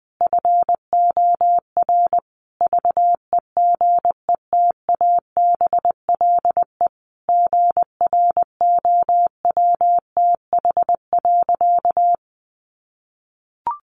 Morse code, heard as a keyed tone, about 20 wpm, 700 Hz. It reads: FOR VEGETABLE GROWTH.
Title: Sentences from Top 1300 Words 20wpm